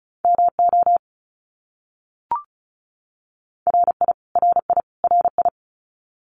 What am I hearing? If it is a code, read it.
Morse code, heard as a keyed tone, about 34 wpm, 700 Hz. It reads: MO RI RI RI
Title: US States Repeated 3x 35wpm